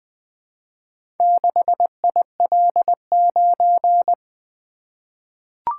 Morse code, heard as a keyed tone, about 20 wpm, 700 Hz. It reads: 6IL9
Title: Four Letter-Number Combinations - 20wpm